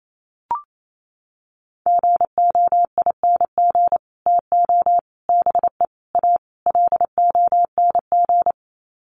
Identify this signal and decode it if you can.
Morse code, heard as a keyed tone, about 28 wpm, 700 Hz. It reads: GOING TO BE A LONG
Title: Sets of 5 Words Encore2 28wpm